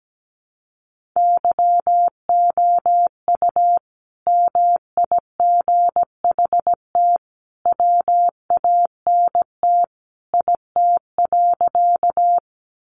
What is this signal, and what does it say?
Morse code, heard as a keyed tone, about 17 wpm, 700 Hz. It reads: YOU MIGHT WANT IT.
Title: Sentences from Top 300 Words 17wpm